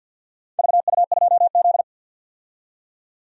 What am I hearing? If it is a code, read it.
Morse code, heard as a keyed tone, about 51 wpm, 700 Hz. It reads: VU2Z
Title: DXCC-100 Call Signs - 4 Characters - 50wpm